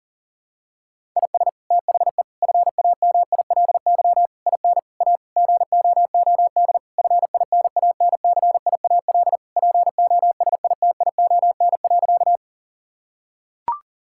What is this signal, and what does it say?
Morse code, heard as a keyed tone, about 37 wpm, 700 Hz. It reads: IS THE FAMILY IN A GOOD FINANCIAL POSITION.
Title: Sentences from Top 1000 Words - Review Part 5 - 40wpm